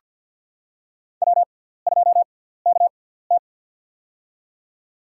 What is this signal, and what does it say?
Morse code, heard as a keyed tone, about 47 wpm, spaced as for 17 wpm, 700 Hz. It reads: W2KT
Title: USA Call Signs - 4 Characters - 6x Character Spacing - 50wpm